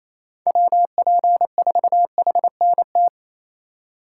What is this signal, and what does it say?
Morse code, heard as a keyed tone, about 28 wpm, 700 Hz. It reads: WP4HNT
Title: USA Call Signs - 6 Characters - 3x Character Spacing - 28wpm